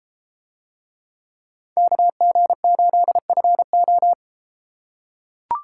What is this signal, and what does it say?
Morse code, heard as a keyed tone, about 32 wpm, 700 Hz. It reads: KG8FO